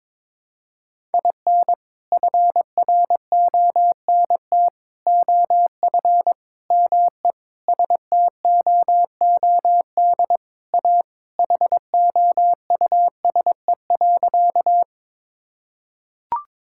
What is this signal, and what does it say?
Morse code, heard as a keyed tone, about 22 wpm, 700 Hz. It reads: IN FRONT OF ME STOOD A HOUSE.
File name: practice / 22wpm / Sentences with 4-Word Phrases - 22wpm